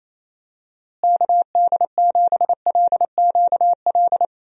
Morse code, heard as a keyed tone, about 28 wpm, 700 Hz. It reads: KD7LQL
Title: USA Call Signs - 6 Characters - Repeated 2x - 28wpm